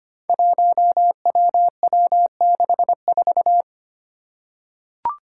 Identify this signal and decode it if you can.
Morse code, heard as a keyed tone, about 25 wpm, 700 Hz. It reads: 1WW64
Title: Five Challenging Letter-Number Combinations 25wpm